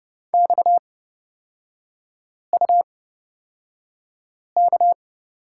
Morse code, heard as a keyed tone, about 30 wpm, spaced as for 5 wpm, 700 Hz. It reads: XUK
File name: Lesson 37e - Letter x - Three Characters 30wpm